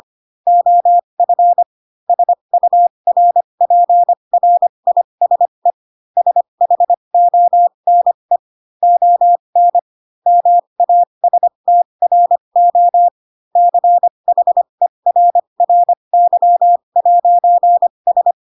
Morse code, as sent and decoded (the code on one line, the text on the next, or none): --- ..-. / ... ..- .-. .--. .-. .. ... / ... .... --- -. / --- -. / -- .- ... - .-. --- / -.-. .... .-. .-. -.-- .----. ...
OF SURPRISE SHONE ON MASTRO CHERRY'S